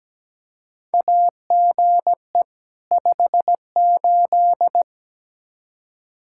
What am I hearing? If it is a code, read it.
Morse code, heard as a keyed tone, about 17 wpm, 700 Hz. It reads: AGE 58
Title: Practice QSO - Exchange 3 - More Info - 17wpm